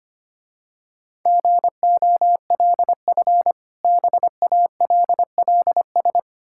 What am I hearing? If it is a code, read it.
Morse code, heard as a keyed tone, about 25 wpm, 700 Hz. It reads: GOLF BALLS
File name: practice / 25wpm / Difficult Words with Hints - 25wpm